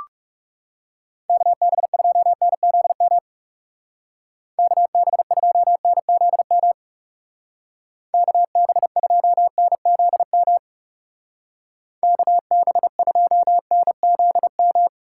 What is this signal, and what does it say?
Morse code, heard as a keyed tone, about 37 wpm, 700 Hz. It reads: KB2NZM KB2NZM KB2NZM KB2NZM
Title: Speed Racing - USA 2x3 Call Signs 30wpm